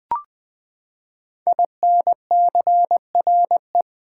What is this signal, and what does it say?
Morse code, heard as a keyed tone, about 20 wpm, 700 Hz. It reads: INCRE
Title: Build A Word - 5 Letters At A Time - 12-Letter Words - 20wpm